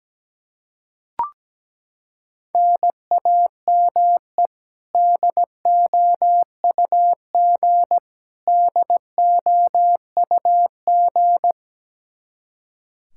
Morse code, heard as a keyed tone, about 17 wpm, 700 Hz. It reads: NAME DOUG DOUG
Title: Practice QSO - Exchange 1 - Beginning - 17wpm